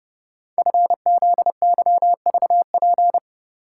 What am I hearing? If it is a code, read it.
Morse code, heard as a keyed tone, about 30 wpm, 700 Hz. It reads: FZYVP
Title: Code Groups - 5 Letter:Numbers - 30wpm